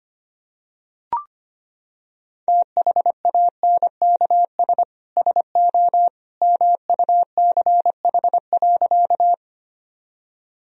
Morse code, heard as a keyed tone, about 25 wpm, 700 Hz. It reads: THANKS SO MUCH.
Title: Sentences from Top 1000 Words - Review Part 3 - 25wpm